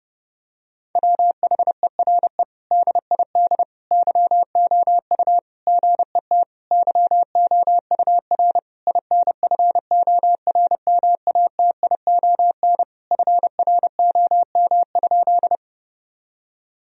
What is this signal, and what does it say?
Morse code, heard as a keyed tone, about 30 wpm, 700 Hz. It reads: WHERE DID YOU GET YOUR INFORMATION FROM?
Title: Sentences from Top 1000 Words - Review Part 3 - 30wpm